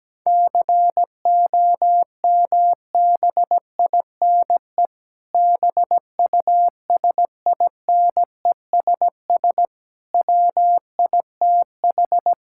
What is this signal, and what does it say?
Morse code, heard as a keyed tone, about 17 wpm, 700 Hz. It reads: COMBINE BUSINESS WITH
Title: Sentences from Top 1600 Words 17wpm